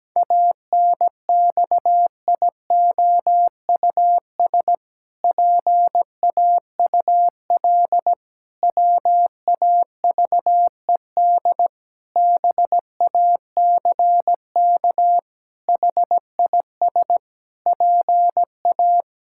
Morse code, heard as a keyed tone, about 17 wpm, 700 Hz. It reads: ANXIOUS PAUL WAVED BACK HIS PA
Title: Pangram Challenge v2 - 17wpm